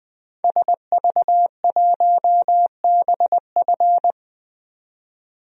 Morse code, heard as a keyed tone, about 20 wpm, 700 Hz. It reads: SV1BF
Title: DXCC-100 Call Signs - 5 Characters - 20wpm